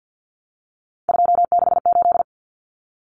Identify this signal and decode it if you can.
Morse code, heard as a keyed tone, about 50 wpm, 700 Hz. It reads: 268
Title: Common Cut Numbers - 3-Digit 50wpm